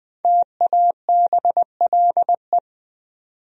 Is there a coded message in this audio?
Yes: Morse code from 0.2 to 2.6 s, heard as a keyed tone, about 19 wpm, 700 Hz. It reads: TABLE